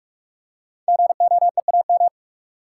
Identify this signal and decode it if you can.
Morse code, heard as a keyed tone, about 44 wpm, 700 Hz. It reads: GOEAM
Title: Code Groups - 5 Letters - 45wpm